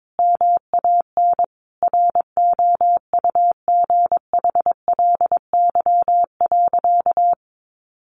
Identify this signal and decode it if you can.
Morse code, heard as a keyed tone, about 22 wpm, 700 Hz. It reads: MAN ROUGHLY.
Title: Sentences from Top 1500 Words 22wpm